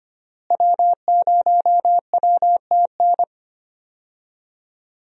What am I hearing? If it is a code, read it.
Morse code, heard as a keyed tone, about 25 wpm, 700 Hz. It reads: W0WTN
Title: USA Call Signs - 5 Characters - Repeated 3x - 25wpm